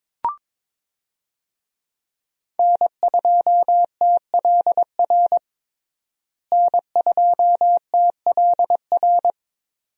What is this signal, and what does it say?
Morse code, heard as a keyed tone, about 22 wpm, 700 Hz. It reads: N2TLR N2TLR
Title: USA Call Signs - 5 Characters - Repeated 2x - 22wpm